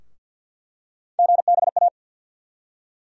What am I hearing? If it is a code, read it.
Morse code, heard as a keyed tone, about 47 wpm, 700 Hz. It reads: GBA
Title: USA Call Sign Postfixes - 3 Letters 50wpm